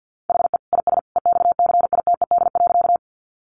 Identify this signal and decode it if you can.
Morse code, heard as a keyed tone, about 50 wpm, 700 Hz. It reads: HE IS EXCITED.
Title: Additional Easy Phrases 50wpm